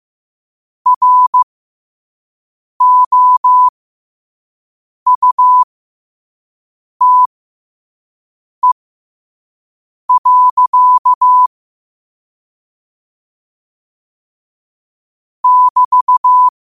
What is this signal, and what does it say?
Morse code, heard as a keyed tone, about 15 wpm, spaced as for 5 wpm, 1000 Hz. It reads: ROUTE. <BT>
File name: qotdfiles / 5wpm